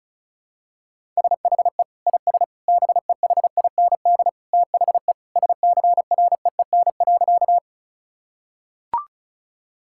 Morse code, heard as a keyed tone, about 34 wpm, 700 Hz. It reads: SHE IS BEHIND THE SCREEN.
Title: Sentences from Top 1000 Words - Review Part 6 - 35wpm